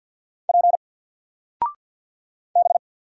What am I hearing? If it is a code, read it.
Morse code, heard as a keyed tone, about 49 wpm, 700 Hz. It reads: P B
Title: Single Character (Letter:Number:Punctuation:Prosign) 50wpm